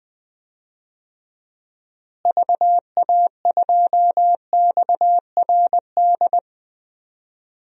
Morse code, heard as a keyed tone, about 20 wpm, 700 Hz. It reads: VA2XRD
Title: DXCC-100 Call Signs - 6 Characters - Repeated 3x - 20wpm